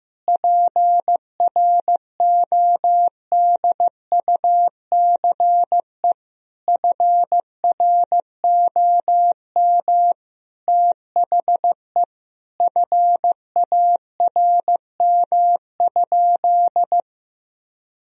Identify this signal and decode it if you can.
Morse code, heard as a keyed tone, about 15 wpm, 700 Hz. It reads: PRODUCE FROM THE FARM?
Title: Sentences from Top 1000 Words - Review Part 3 - 15wpm